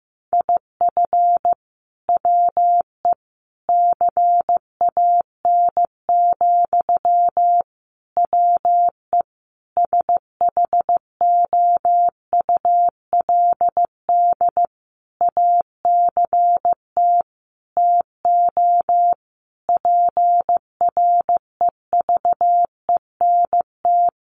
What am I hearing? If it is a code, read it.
Morse code, heard as a keyed tone, about 15 wpm, 700 Hz. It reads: IF WE CAN, WE SHOULD ACT TO PREVENT
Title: Sentences from Top 1100 Words 15wpm